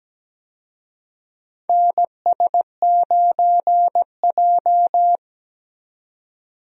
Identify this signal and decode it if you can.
Morse code, heard as a keyed tone, about 17 wpm, 700 Hz. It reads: NS9J
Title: USA Call Signs - 4 Characters - 17wpm